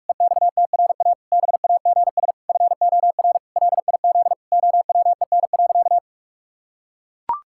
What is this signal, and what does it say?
Morse code, heard as a keyed tone, about 41 wpm, 700 Hz. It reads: EXTRA BAGS FOR LIZ OWEN.
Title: Pangram Challenge v2 - 45wpm